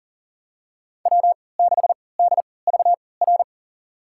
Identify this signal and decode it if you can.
Morse code, heard as a keyed tone, about 40 wpm, spaced as for 23 wpm, 700 Hz. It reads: W6DVR
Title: USA Call Signs - 5 Characters - 3x Character Spacing - 40wpm